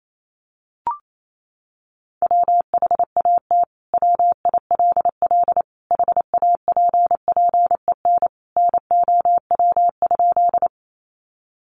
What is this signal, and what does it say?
Morse code, heard as a keyed tone, about 28 wpm, 700 Hz. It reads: WHAT WILL HAPPEN NOW?
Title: Sentences from Top 400 Words 28wpm